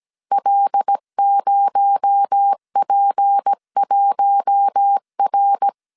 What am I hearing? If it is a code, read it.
Morse code, heard as a keyed tone, about 17 wpm, 800 Hz. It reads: L0P1R
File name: 16wpm